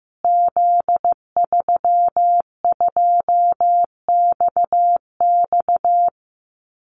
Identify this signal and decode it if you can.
Morse code, heard as a keyed tone, about 15 wpm, 700 Hz. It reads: Z32XX
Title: Call Signs Repeated 2x - 15wpm